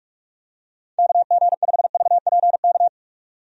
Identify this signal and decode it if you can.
Morse code, heard as a keyed tone, about 44 wpm, 700 Hz. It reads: KG5VPK
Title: USA Call Signs - 6 Characters - 45wpm